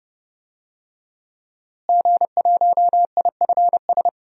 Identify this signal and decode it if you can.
Morse code, heard as a keyed tone, about 30 wpm, 700 Hz. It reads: G1IFS
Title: DXCC-100 Call Signs - 5 Characters - Repeated 3x - 30wpm